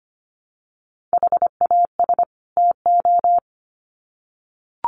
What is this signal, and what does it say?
Morse code, heard as a keyed tone, about 25 wpm, 700 Hz. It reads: HAS TO
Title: Sets of 2 Words 25wpm